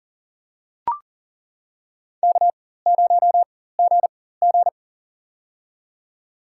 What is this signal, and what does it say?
Morse code, heard as a keyed tone, about 40 wpm, spaced as for 19 wpm, 700 Hz. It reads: K0GG
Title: USA Call Signs - 4 Characters - 4x Character Spacing - 40wpm